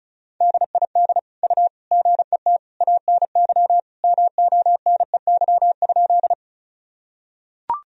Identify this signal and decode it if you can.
Morse code, heard as a keyed tone, about 34 wpm, 700 Hz. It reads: DID U GET ANY MONEY?
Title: Sentences from Top 400 Words and Abbreviations 35wpm